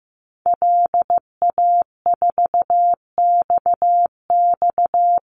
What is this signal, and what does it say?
Morse code, heard as a keyed tone, about 15 wpm, 700 Hz. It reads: LA4XX
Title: Call Signs Repeated 3x - 15wpm